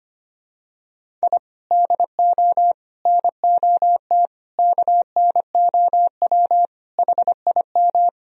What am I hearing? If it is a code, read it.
Morse code, heard as a keyed tone, about 25 wpm, 700 Hz. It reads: I DO NOT KNOW HIM
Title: Sentences from Top 1000 Words - Review Part 1 - 25wpm